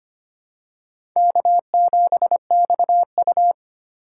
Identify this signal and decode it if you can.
Morse code, heard as a keyed tone, about 25 wpm, 700 Hz. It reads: K7XU
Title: USA Call Signs - 4 Characters - 6x Character Spacing - 25wpm